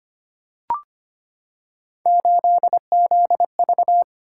Morse code, heard as a keyed tone, about 25 wpm, 700 Hz. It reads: 8ZV
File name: Three Letter-Number Combinations - 25wpm